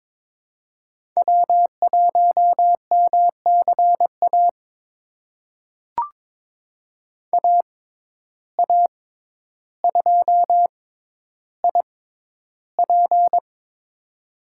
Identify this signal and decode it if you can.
Morse code, heard as a keyed tone, about 22 wpm, spaced as for 11 wpm, 700 Hz. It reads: W1MCA AA2IP
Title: USA Call Signs - 5 Characters - 6x Character Spacing - 22wpm